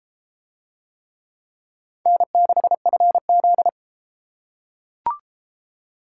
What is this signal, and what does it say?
Morse code, heard as a keyed tone, about 31 wpm, 700 Hz. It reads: N6FZ